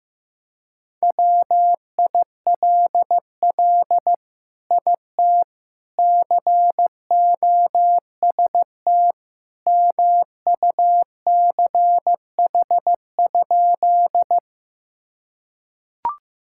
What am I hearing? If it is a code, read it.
Morse code, heard as a keyed tone, about 15 wpm, 700 Hz. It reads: WILL IT COST MUCH?
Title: Sentences from Top 1000 Words - Review Part 5 - 15wpm